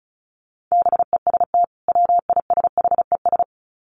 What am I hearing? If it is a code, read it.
Morse code, heard as a keyed tone, about 34 wpm, 700 Hz. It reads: BEST WISHES
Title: Common QSO Phrases 35wpm